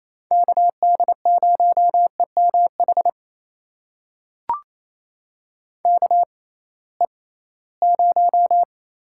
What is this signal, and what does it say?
Morse code, heard as a keyed tone, about 28 wpm, spaced as for 18 wpm, 700 Hz. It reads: KD0EMH KE0
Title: USA Call Signs - 6 Characters - 6x Character Spacing - 28wpm